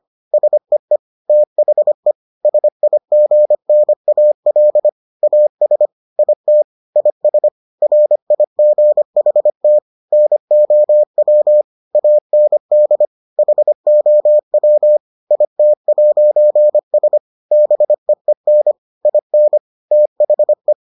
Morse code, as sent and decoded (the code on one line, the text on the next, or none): ... . / - .... / ... .. --. -. .- .-.. / .- ... / .. - / .. ... / .-. .. --. .... - / -. --- .-- / .- -. -.. / .... --- .-- / .. - .----. ... / -... . -. / .. -. / - .... .
SEE THE SIGNAL AS IT IS RIGHT NOW AND HOW IT'S BEEN IN THE